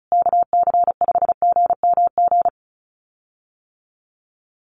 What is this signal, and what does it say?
Morse code, heard as a keyed tone, about 34 wpm, 700 Hz. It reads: KC5GMG
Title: USA Call Signs Repeated 3x 35wpm